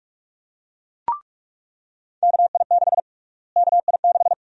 Morse code, heard as a keyed tone, about 46 wpm, 700 Hz. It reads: KI6 KI6
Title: USA Call Sign Prefixes - Repeated 2x - 45wpm